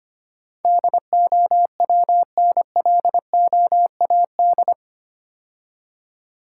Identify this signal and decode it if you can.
Morse code, heard as a keyed tone, about 25 wpm, 690 Hz. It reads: DOWNLOAD
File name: 8 Letter Words - First 3 Letters Given - 25wpm